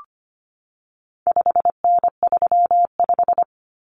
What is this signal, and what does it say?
Morse code, heard as a keyed tone, about 25 wpm, 700 Hz. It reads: 5N35
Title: Common Cut Numbers - 4-Digit 25wpm